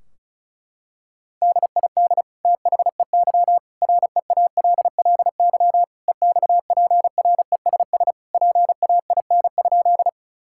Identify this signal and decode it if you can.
Morse code, heard as a keyed tone, about 36 wpm, 700 Hz. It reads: DID THEY REALLY EXPRESS PAIN?
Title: Sentences from Top 1600 Words 35wpm